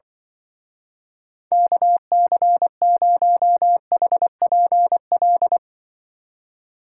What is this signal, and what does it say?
Morse code, heard as a keyed tone, about 24 wpm, 700 Hz. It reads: KC0HPL
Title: Speed Racing - USA 2x3 Call Signs 20wpm